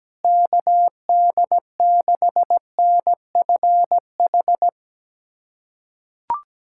Morse code, heard as a keyed tone, about 17 wpm, 700 Hz. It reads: KD6NFH
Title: USA Callsign DE Callsign - 6 Characters - 17wpm